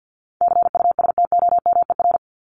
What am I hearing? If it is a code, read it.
Morse code, heard as a keyed tone, about 54 wpm, 700 Hz. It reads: CUSTOMER
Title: Top 1001-1250 Words 50wpm